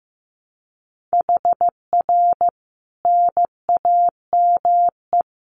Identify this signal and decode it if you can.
Morse code, heard as a keyed tone, about 15 wpm, 700 Hz. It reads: HR NAME
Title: Common QSO Phrases 15wpm